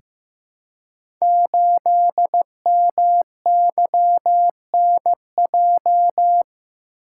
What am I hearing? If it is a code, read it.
Morse code, heard as a keyed tone, about 15 wpm, 700 Hz. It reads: 8MYNJ
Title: Code Groups - 5 Letter:Numbers - 15wpm